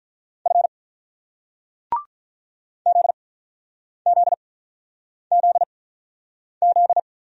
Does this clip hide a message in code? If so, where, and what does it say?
Yes: Morse code from 0.5 to 7.0 s, heard as a keyed tone, about 46 wpm, spaced as for 9 wpm, 700 Hz. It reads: F ZZZZ